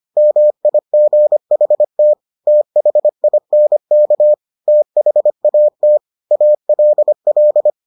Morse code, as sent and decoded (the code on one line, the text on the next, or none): -- .. --. .... - / - .... .. -. -.- / - .... .- - / .- .-.. .-..
MIGHT THINK THAT ALL